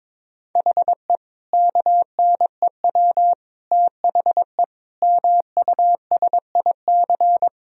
Morse code, heard as a keyed tone, about 22 wpm, 700 Hz. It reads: HE KNEW THE MUSIC
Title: Sentences from Top 1000 Words - Review Part 1 - 22wpm